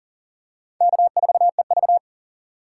Morse code, heard as a keyed tone, about 37 wpm, 700 Hz. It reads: K4EV
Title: USA Call Signs - 3x Character Spacing - 40wpm